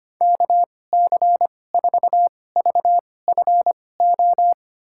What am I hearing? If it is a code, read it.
Morse code, heard as a keyed tone, about 25 wpm, spaced as for 18 wpm, 700 Hz. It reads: KC4VFO
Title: USA Call Signs - 2x Character Spacing - 25wpm